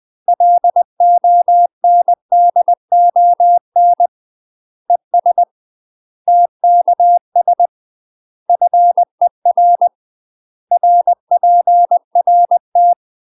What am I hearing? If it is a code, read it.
Morse code, heard as a keyed tone, about 20 wpm, 700 Hz. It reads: LONDON ES TKS FER RPRT
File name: UK MASTER QSOs TRAINING / 20WPM / UK MASTER QSOs 0008